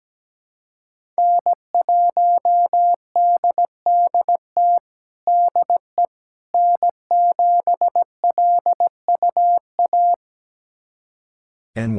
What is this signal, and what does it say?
Morse code, heard as a keyed tone, about 17 wpm, 700 Hz. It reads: N1DDT DE N7LUA
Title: USA Callsign DE Callsign - 5 Characters - 17wpm